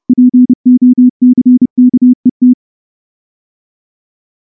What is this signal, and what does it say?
Morse code, heard as a keyed tone, about 28 wpm, 250 Hz. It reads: POCKET
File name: morse-training-30WPM